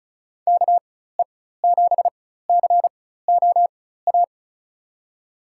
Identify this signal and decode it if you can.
Morse code, heard as a keyed tone, about 36 wpm, spaced as for 16 wpm, 700 Hz. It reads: KE7COA
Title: USA Call Signs - 6 Characters - 4x Character Spacing - 35wpm